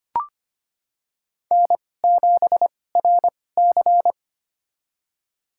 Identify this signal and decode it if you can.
Morse code, heard as a keyed tone, about 25 wpm, spaced as for 18 wpm, 700 Hz. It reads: N7RC
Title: USA Call Signs - 4 Characters - 2x Character Spacing - 25wpm